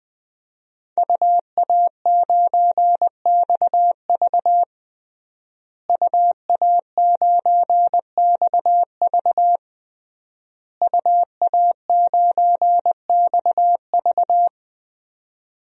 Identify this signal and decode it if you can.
Morse code, heard as a keyed tone, about 21 wpm, 700 Hz. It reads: UA9XV UA9XV UA9XV
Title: DXCC-100 Call Signs - 5 Characters - Repeated 3x - 20wpm